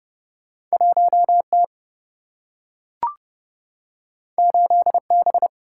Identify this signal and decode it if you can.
Morse code, heard as a keyed tone, about 30 wpm, 700 Hz. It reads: 1T 8B
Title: Single-Letter Single-Number 30wpm